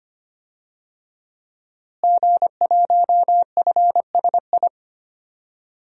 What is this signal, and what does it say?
Morse code, heard as a keyed tone, about 25 wpm, 700 Hz. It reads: G1FSI
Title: DXCC-100 Call Signs - 5 Characters - Repeated 2x - 25wpm